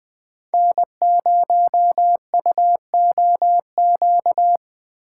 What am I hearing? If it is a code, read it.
Morse code, heard as a keyed tone, about 20 wpm, 700 Hz. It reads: N0UOQ
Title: USA Call Signs - 5 Characters - 3x Character Spacing - 20wpm